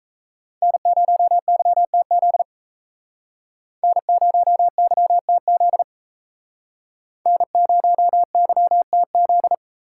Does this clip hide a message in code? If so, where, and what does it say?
Yes: Morse code from 0.6 to 9.5 s, heard as a keyed tone, about 37 wpm, 700 Hz. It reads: N0YTZ N0YTZ N0YTZ